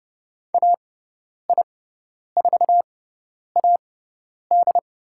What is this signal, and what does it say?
Morse code, heard as a keyed tone, about 29 wpm, spaced as for 10 wpm, 700 Hz. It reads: AI4AD
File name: Lesson 17g - Letter u - Call Signs 30wpm